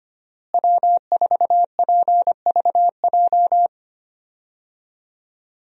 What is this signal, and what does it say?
Morse code, heard as a keyed tone, about 25 wpm, 700 Hz. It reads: W4PVJ
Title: USA Call Signs - 5 Characters - Repeated 2x - 25wpm